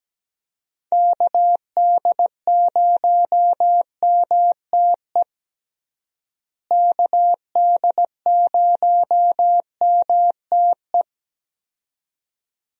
Morse code, heard as a keyed tone, about 17 wpm, 700 Hz. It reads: KD0MTE KD0MTE
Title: USA Call Signs - 6 Characters - Repeated 2x - 17wpm